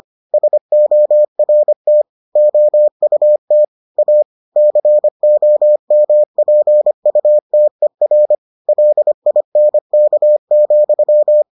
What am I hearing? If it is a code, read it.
Morse code, heard as a keyed tone, about 25 wpm, 600 Hz. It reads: SORT OUT A COMPUTER LINK,